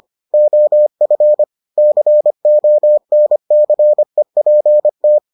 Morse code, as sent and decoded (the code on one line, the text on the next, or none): --- ..-. / -.-. --- -. -.-. .--. -
OF CONCEPT